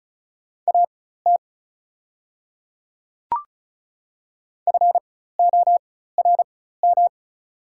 Morse code, heard as a keyed tone, about 36 wpm, spaced as for 16 wpm, 690 Hz. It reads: AT FORM
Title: 4x Character Spacing - Top 200 Words - 35wpm